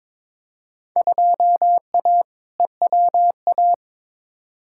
Morse code, heard as a keyed tone, about 22 wpm, 700 Hz. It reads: 2A EWA
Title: Field Day Exchanges - 22wpm